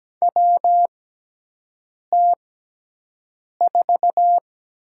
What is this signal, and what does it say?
Morse code, heard as a keyed tone, about 17 wpm, spaced as for 6 wpm, 700 Hz. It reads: WT4
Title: USA Call Signs - 4 Characters - 6x Character Spacing - 17wpm